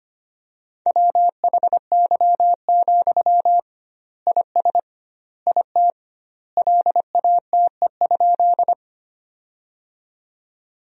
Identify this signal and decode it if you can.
Morse code, heard as a keyed tone, about 25 wpm, 700 Hz. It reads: WHY, IS IT LATE?
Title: 2x Word-Spacing - Sentences from Top 300 Words - 25wpm